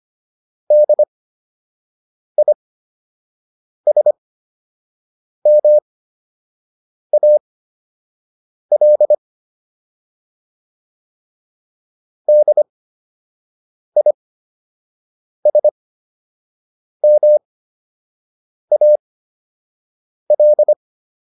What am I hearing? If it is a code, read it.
Morse code, heard as a keyed tone, about 25 wpm, spaced as for 6 wpm, 600 Hz. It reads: DISMAL DISMAL